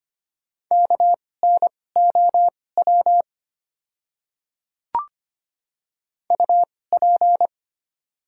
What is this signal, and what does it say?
Morse code, heard as a keyed tone, about 25 wpm, spaced as for 18 wpm, 690 Hz. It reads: KNOW UP
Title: 2x Character Spacing - Top 100 Words - 25wpm